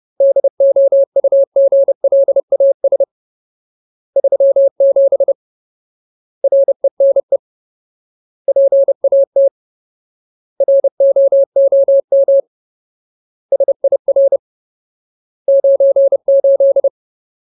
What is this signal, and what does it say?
Morse code, heard as a keyed tone, about 30 wpm, 550 Hz. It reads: DOUGLAS 37 RENE PAT ROOM SIR 98
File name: Ultimate Word List / 30WPM Ultimate